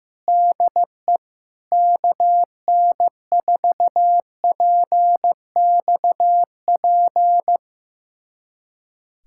Morse code, heard as a keyed tone, about 15 wpm, 700 Hz. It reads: DE KN4PXP
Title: USA Callsign DE Callsign 15wpm